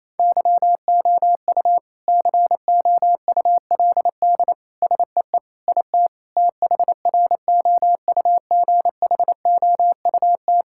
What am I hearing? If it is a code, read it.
Morse code, heard as a keyed tone, about 27 wpm, 700 Hz. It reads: YOU COULD SEE IT THROUGHOUT
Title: Sentences from Top 1000 Words - Review Part 5 - 28wpm